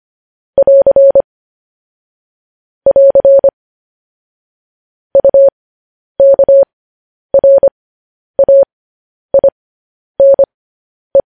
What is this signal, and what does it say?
Morse code, heard as a keyed tone, about 25 wpm, spaced as for 10 wpm, 550 Hz. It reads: <AR> <AR> UKRAINE
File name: podcast / News headlines-1x-25x10